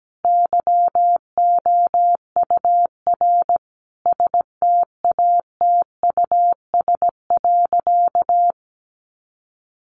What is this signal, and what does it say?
Morse code, heard as a keyed tone, about 17 wpm, 700 Hz. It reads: YOUR STATUS.
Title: Sentences from Top 1400 Words 17wpm